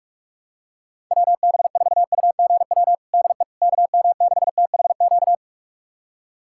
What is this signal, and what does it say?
Morse code, heard as a keyed tone, about 46 wpm, 700 Hz. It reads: WB4UGW DE KM6THQ
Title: USA Callsign DE Callsign - 6 Characters - 45wpm